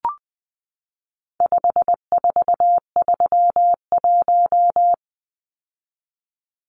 Morse code, heard as a keyed tone, about 20 wpm, 700 Hz. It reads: 5431
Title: Numbers - 4-Digit 20wpm